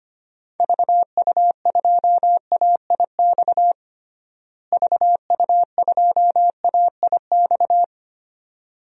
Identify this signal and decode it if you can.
Morse code, heard as a keyed tone, about 25 wpm, 700 Hz. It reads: VU2AIX VU2AIX
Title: DXCC-100 Call Signs - 6 Characters - Repeated 2x - 25wpm